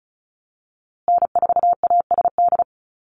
Morse code, heard as a keyed tone, about 34 wpm, 700 Hz. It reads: N4ASD
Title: USA Call Signs Repeated 2x 35wpm